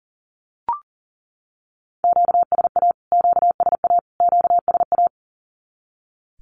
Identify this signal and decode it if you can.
Morse code, heard as a keyed tone, about 40 wpm, 700 Hz. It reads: QSA QSA QSA
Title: Q-Codes Repeated 3x 40wpm